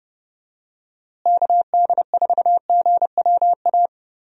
Morse code, heard as a keyed tone, about 30 wpm, 700 Hz. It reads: KD4GWA
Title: USA Call Signs - 6 Characters - Repeated 2x - 30wpm